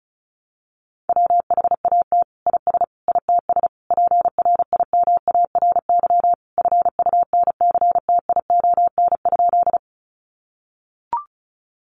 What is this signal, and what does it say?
Morse code, heard as a keyed tone, about 34 wpm, 700 Hz. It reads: WHAT IS ITS PRIMARY FUNCTION?
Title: Sentences from Top 1100 Words 35wpm